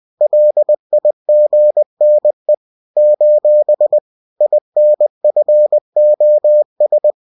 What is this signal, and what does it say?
Morse code, heard as a keyed tone, about 20 wpm, 600 Hz. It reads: LIGNE : INFOS